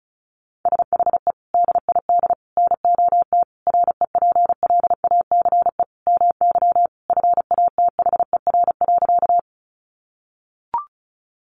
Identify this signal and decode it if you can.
Morse code, heard as a keyed tone, about 34 wpm, 700 Hz. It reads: SHE DID NOT REPLACE MY FATHER.
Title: Sentences from Top 1400 Words 35wpm